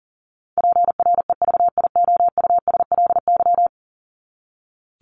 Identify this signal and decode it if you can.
Morse code, heard as a keyed tone, about 40 wpm, 700 Hz. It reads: PREVIOUSLY
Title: LY Suffix - Top 100 Words with Base Words 40wpm